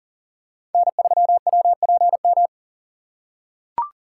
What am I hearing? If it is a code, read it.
Morse code, heard as a keyed tone, about 40 wpm, 700 Hz. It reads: N3WPM
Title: USA Call Signs - 5 Characters - 2x Character Spacing - 40wpm